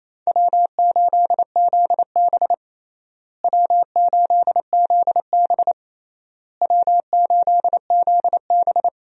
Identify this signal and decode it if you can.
Morse code, heard as a keyed tone, about 28 wpm, 700 Hz. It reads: W8ZB W8ZB W8ZB
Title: USA Call Signs - 4 Characters - Repeated 3x - 28wpm